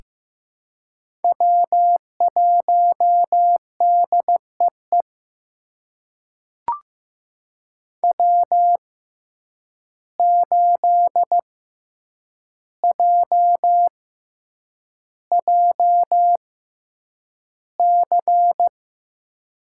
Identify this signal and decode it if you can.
Morse code, heard as a keyed tone, about 15 wpm, spaced as for 8 wpm, 700 Hz. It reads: W1DEE W8JJC
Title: USA Call Signs - 5 Characters - 6x Character Spacing - 15wpm